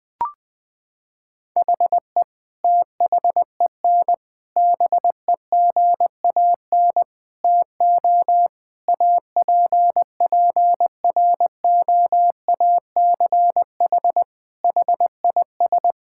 Morse code, heard as a keyed tone, about 20 wpm, 700 Hz. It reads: HE THEN BEGAN TO APPROACH HIS
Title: Sentences from Top 1000 Words - Review Part 6 - 20wpm